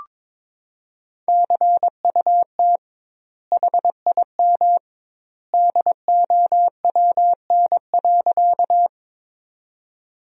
Morse code, heard as a keyed tone, about 22 wpm, 700 Hz. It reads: CUT HIM DOWN.
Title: 2x Word-Spacing - Sentences from Top 300 Words - 22wpm